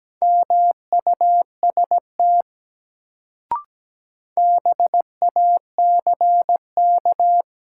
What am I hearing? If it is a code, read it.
Morse code, heard as a keyed tone, about 17 wpm, 700 Hz. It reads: MUST BACK
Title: Sets of 2 Words - New Word Drill - 17wpm